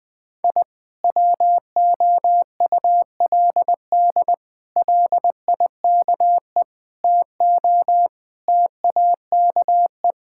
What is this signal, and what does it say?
Morse code, heard as a keyed tone, about 20 wpm, 700 Hz. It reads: I WOULD LIKE TO TAKE
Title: Sentences with 8-Word Phrases - 20wpm